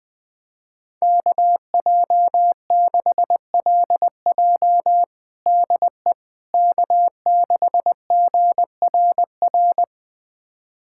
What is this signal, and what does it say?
Morse code, heard as a keyed tone, about 20 wpm, 700 Hz. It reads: KJ6LJ DE K6GRR
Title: USA Callsign DE Callsign - 5 Characters - 20wpm